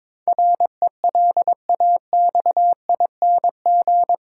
Morse code, heard as a keyed tone, about 22 wpm, 700 Hz. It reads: RELAXING
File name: practice / 22wpm / Eight Letter Words - 22wpm